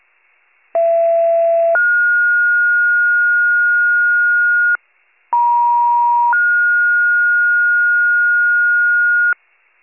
Two-tone page, heard as a one-second tone, then a three-second tone, about 670 then 1430 Hz, about 950 then 1430 Hz. It is heard from 0.7 to 9.3 s.